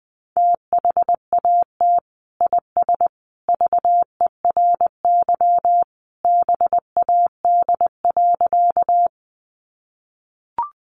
Morse code, heard as a keyed tone, about 20 wpm, 700 Hz. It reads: THAT IS VERY BAD.
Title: Sentences from Top 700 Words 20wpm